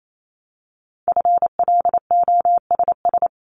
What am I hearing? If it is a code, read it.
Morse code, heard as a keyed tone, about 28 wpm, 700 Hz. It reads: FLOSS
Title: Multi-letter Phonemes 28wpm